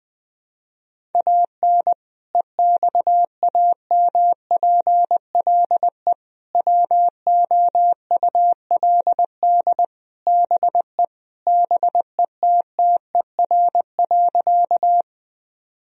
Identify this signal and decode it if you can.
Morse code, heard as a keyed tone, about 20 wpm, 700 Hz. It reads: AN EXAMPLE WOULD BE BETTER.
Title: Sentences from Top 1000 Words - Review Part 1 - 20wpm